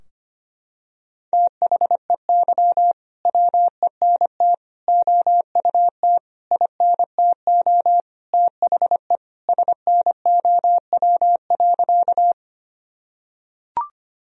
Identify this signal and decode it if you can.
Morse code, heard as a keyed tone, about 25 wpm, 700 Hz. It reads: THEY WENT OUT INTO THE SNOW.
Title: Sentences from Top 1000 Words - Review Part 3 - 25wpm